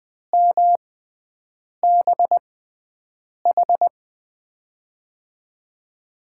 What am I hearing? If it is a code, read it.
Morse code, heard as a keyed tone, about 20 wpm, spaced as for 7 wpm, 700 Hz. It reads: MBH
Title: USA Call Signs - 6x Character Spacing - 20wpm